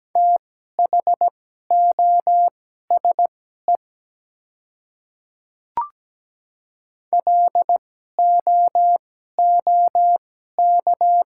Morse code, heard as a keyed tone, about 17 wpm, spaced as for 12 wpm, 700 Hz. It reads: THOSE LOOK
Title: 2x Character Spacing - Top 300 Words - 17wpm